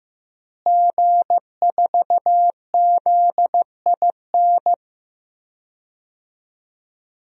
Morse code, heard as a keyed tone, about 15 wpm, 700 Hz. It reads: G4ZIN
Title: DXCC-100 Call Signs - 5 Characters - Repeated 2x - 15wpm